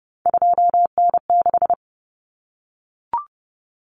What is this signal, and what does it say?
Morse code, heard as a keyed tone, about 30 wpm, 700 Hz. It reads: 2N6
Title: Common Cut Numbers - 3-Digit 30wpm